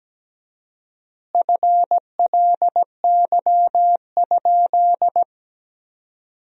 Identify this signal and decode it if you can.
Morse code, heard as a keyed tone, about 17 wpm, 700 Hz. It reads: FLY?
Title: 4x Word-Spacing - Sentences from Top 500 Words - 17wpm